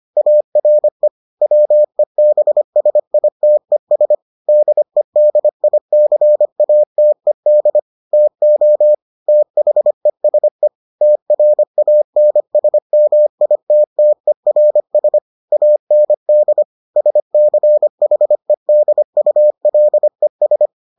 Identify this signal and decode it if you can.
Morse code, heard as a keyed tone, about 25 wpm, 600 Hz. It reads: ARE WEBSITES DEDICATED TO THESE TRANSMITTERS AND SCHEDULES